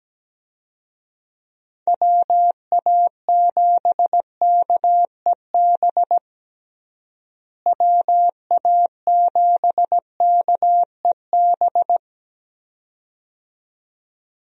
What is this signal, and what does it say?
Morse code, heard as a keyed tone, about 17 wpm, 700 Hz. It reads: WA7KEB WA7KEB
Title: USA Call Signs - 6 Characters - Repeated 2x - 17wpm